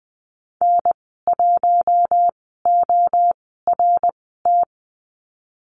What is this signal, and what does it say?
Morse code, heard as a keyed tone, about 20 wpm, spaced as for 15 wpm, 700 Hz. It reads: N1ORT
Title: 128 Lesson 21g - Number 6 - Call Signs 20wpm